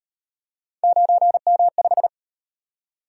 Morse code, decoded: 9M5

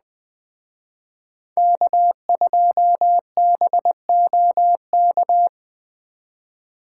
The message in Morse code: K2BOK

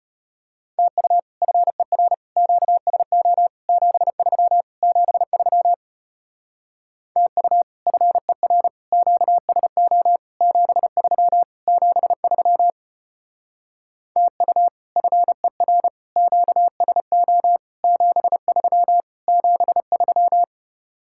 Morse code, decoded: TU FER QSO 73 73 TU FER QSO 73 73 TU FER QSO 73 73

